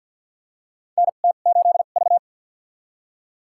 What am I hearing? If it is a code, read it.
Morse code, heard as a keyed tone, about 52 wpm, spaced as for 37 wpm, 700 Hz. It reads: NT8V
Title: USA Call Signs - 4 Characters - 2x Character Spacing - 50wpm